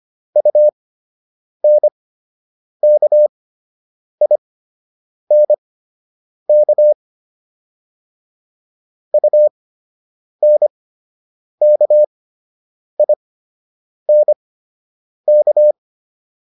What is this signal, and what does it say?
Morse code, heard as a keyed tone, about 25 wpm, spaced as for 8 wpm, 600 Hz. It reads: UNKINK UNKINK